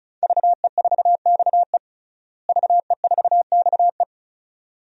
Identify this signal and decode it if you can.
Morse code, heard as a keyed tone, about 34 wpm, 700 Hz. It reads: VE4XE VE4XE
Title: DXCC-100 Call Signs - 5 Characters - Repeated 2x - 35wpm